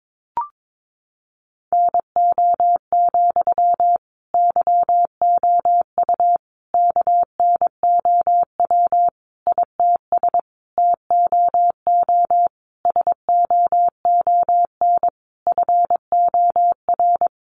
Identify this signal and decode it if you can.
Morse code, heard as a keyed tone, about 22 wpm, 700 Hz. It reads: NO, YOU KNOW ITS TOO SOON FOR
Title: Sentences from Top 800 Words 22wpm